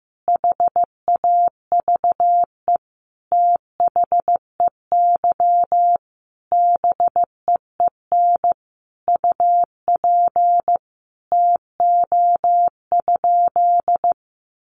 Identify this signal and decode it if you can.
Morse code, heard as a keyed tone, about 14 wpm, 700 Hz. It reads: HAVE THEY BEEN UP TO?
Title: Sentences from Top 100 Words 15wpm